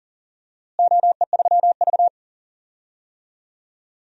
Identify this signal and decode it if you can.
Morse code, heard as a keyed tone, about 40 wpm, 700 Hz. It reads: OE3V
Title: DXCC-100 Call Signs - 4 Characters - Repeated 3x - 40wpm